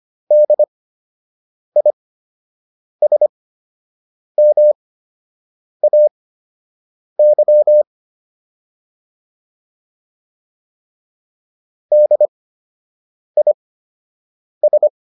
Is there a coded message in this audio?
Yes: Morse code from 0.3 to 14.9 s, heard as a keyed tone, about 25 wpm, spaced as for 7 wpm, 600 Hz. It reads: DISMAY DIS